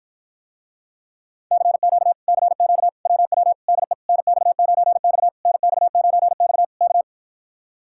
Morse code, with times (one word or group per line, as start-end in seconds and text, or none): 1.5-2.1 CQ
2.3-2.9 CQ
3.1-3.5 WW
3.7-3.9 DE
4.1-5.3 NX9X
5.5-6.6 NX9X
6.8-7.0 K